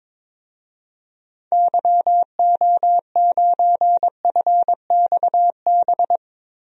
Morse code, heard as a keyed tone, about 22 wpm, 700 Hz. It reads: YO9FXB
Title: DXCC-100 Call Signs - 6 Characters - Repeated 3x - 22wpm